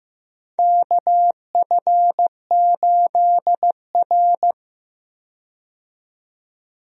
Morse code, heard as a keyed tone, about 15 wpm, 700 Hz. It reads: KF8R
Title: USA Call Signs - 4 Characters - 15wpm